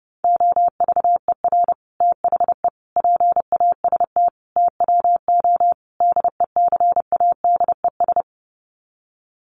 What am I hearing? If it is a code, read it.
Morse code, heard as a keyed tone, about 30 wpm, 700 Hz. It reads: OVER THE PAST TWO DECADES
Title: Sets of 5 Words Encore2 30wpm